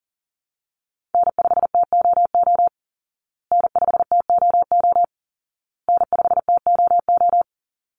Morse code, heard as a keyed tone, about 40 wpm, 700 Hz. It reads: N5TOO N5TOO N5TOO
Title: Call Signs Repeated 2x - 40wpm